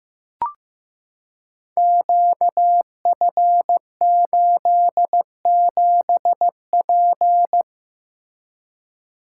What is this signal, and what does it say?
Morse code, heard as a keyed tone, about 15 wpm, 700 Hz. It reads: QF87P
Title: Code Groups - 5 Letter:Numbers - 15wpm